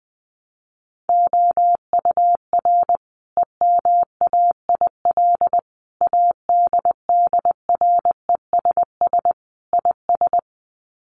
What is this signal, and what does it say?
Morse code, heard as a keyed tone, about 20 wpm, 700 Hz. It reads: OUR EMAIL ADDRESS IS
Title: Sets of 5 Words Encore2 20wpm